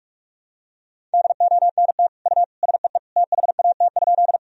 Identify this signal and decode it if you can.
Morse code, heard as a keyed tone, about 44 wpm, 700 Hz. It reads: DONT U SEE THAT?
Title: Sentences from Top 300 Words and Abbreviations 45wpm